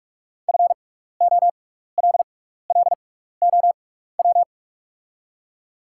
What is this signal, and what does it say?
Morse code, heard as a keyed tone, about 44 wpm, spaced as for 15 wpm, 690 Hz. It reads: FOLLOW
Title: 6x Character Spacing - Top 500 Words - 45wpm